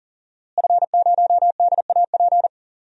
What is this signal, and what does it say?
Morse code, heard as a keyed tone, about 40 wpm, 700 Hz. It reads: F0DAP
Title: DXCC-100 Call Signs - 5 Characters - Repeated 2x - 40wpm